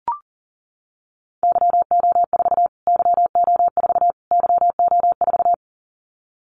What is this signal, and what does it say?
Morse code, heard as a keyed tone, about 40 wpm, 700 Hz. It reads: YO4 YO4 YO4
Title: Callsign Prefixes for Top 25 DXCC - Repeated 3x - 40wpm